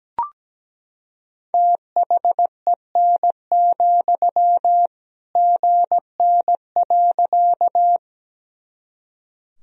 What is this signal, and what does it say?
Morse code, heard as a keyed tone, about 17 wpm, 700 Hz. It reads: THEN, GN.
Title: Sentences from Top 300 Words and Abbreviations 17wpm